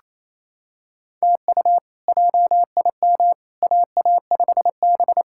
Morse code, heard as a keyed tone, about 27 wpm, 700 Hz. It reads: TU JIM AA5B
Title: K1USN SST QSO Practice - 28wpm